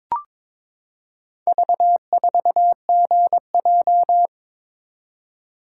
Morse code, heard as a keyed tone, about 22 wpm, 700 Hz. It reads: V4GJ
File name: practice / 22wpm / Four Letter-Number Combinations - 22wpm